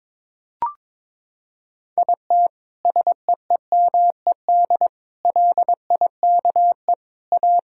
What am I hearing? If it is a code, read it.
Morse code, heard as a keyed tone, about 22 wpm, 700 Hz. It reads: IT SEEMED LIKE A
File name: practice / 22wpm / Sentences with 8-Word Phrases - 22wpm